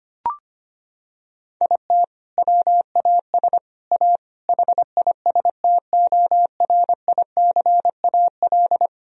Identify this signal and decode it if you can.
Morse code, heard as a keyed tone, about 25 wpm, 700 Hz. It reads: IT WAS A HISTORICAL
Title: Sentences from Top 1000 Words - Review Part 4 - 25wpm